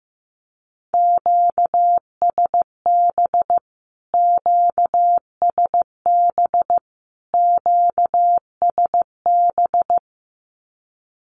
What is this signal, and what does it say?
Morse code, heard as a keyed tone, about 15 wpm, 700 Hz. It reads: QSB QSB QSB
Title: Q-Codes Repeated 3x 15wpm